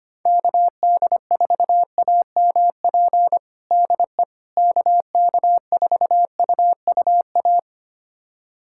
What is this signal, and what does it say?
Morse code, heard as a keyed tone, about 25 wpm, 700 Hz. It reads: KD4AMP DE KK4UUA
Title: USA Callsign DE Callsign 25wpm